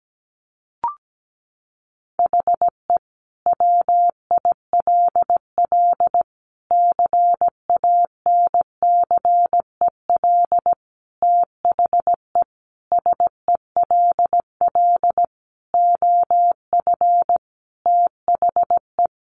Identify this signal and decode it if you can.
Morse code, heard as a keyed tone, about 17 wpm, 700 Hz. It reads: HE WILL CANCEL THE SELL OF THE
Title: Sentences from Top 1200 Words 17wpm